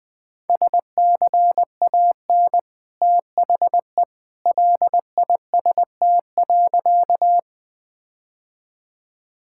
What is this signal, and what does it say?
Morse code, heard as a keyed tone, about 21 wpm, 700 Hz. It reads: SCAN THE LIST.
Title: Easy Sentences with Hints - 20wpm